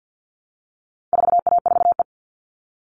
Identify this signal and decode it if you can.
Morse code, heard as a keyed tone, about 50 wpm, 700 Hz. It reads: 4A4E
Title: All Cut Numbers - 4-Digit 50wpm